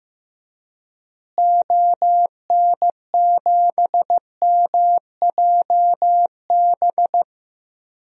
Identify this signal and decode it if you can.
Morse code, heard as a keyed tone, about 15 wpm, 700 Hz. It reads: ON7MJB
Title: DXCC-100 Call Signs - 6 Characters - Repeated 3x - 15wpm